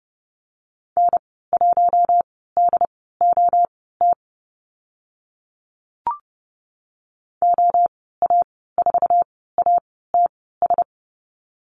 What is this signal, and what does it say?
Morse code, heard as a keyed tone, about 30 wpm, spaced as for 17 wpm, 700 Hz. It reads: N1DOT OA4ATS
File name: Lesson 14g - Number 2 - Call Signs 30wpm